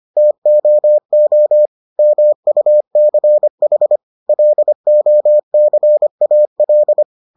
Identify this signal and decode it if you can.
Morse code, heard as a keyed tone, about 25 wpm, 600 Hz. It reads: TOO MUCH LOCAL